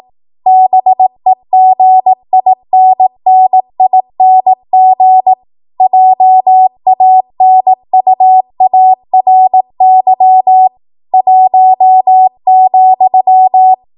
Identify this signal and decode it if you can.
Morse code, heard as a keyed tone, about 18 wpm, 750 Hz. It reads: BEGINNING JANUARY 1,